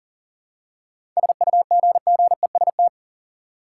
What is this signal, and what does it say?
Morse code, heard as a keyed tone, about 40 wpm, 700 Hz. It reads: SUGGEST
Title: Seven Letter Words - 40wpm